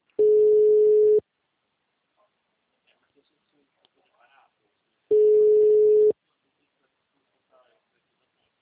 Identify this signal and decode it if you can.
Ringback tone (Europe, 425 Hz), with speech.